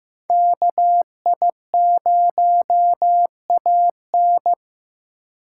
Morse code, heard as a keyed tone, about 15 wpm, 700 Hz. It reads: KI0AN
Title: USA Call Signs - 5 Characters - 4x Character Spacing - 15wpm